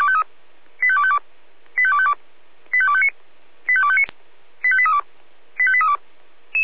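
Five-tone selective call (ZVEI), heard as quick runs of five tones, tones of about 70 ms. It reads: 85252 85252 85259 85259 85832 85832
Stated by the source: ZVEI five-tone selective call 85250 received by SDR.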